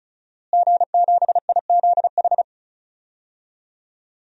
Morse code, heard as a keyed tone, about 36 wpm, 700 Hz. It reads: G7IZH
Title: DXCC-100 Call Signs - 5 Characters - Repeated 2x - 35wpm